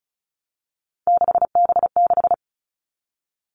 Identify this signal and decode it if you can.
Morse code, heard as a keyed tone, about 34 wpm, 700 Hz. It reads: 6B6
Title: All Cut Numbers - 3-Digit 35wpm